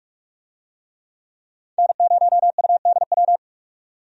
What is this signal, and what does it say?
Morse code, heard as a keyed tone, about 43 wpm, 700 Hz. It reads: N0UDW
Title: USA Call Signs - 5 Characters - 45wpm